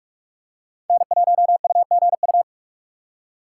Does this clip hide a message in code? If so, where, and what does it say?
Yes: Morse code from 0.9 to 2.4 s, heard as a keyed tone, about 40 wpm, 700 Hz. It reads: N1UGU